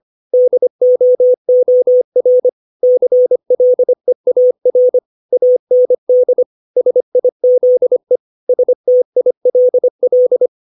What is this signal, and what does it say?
Morse code, heard as a keyed tone, about 25 wpm, 500 Hz. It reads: DOOR CLEAR AND SIZE STILL
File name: 632 words / 632 Words 25WPM